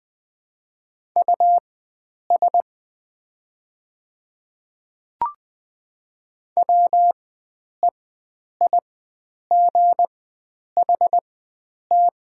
Morse code, heard as a keyed tone, about 20 wpm, spaced as for 9 wpm, 700 Hz. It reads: US WEIGHT
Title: 4x Character Spacing - Top 500 Words - 20wpm